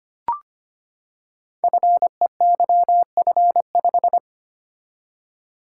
Morse code, heard as a keyed tone, about 25 wpm, 700 Hz. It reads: FEYF5
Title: Code Groups - 5 Letter:Numbers - 25wpm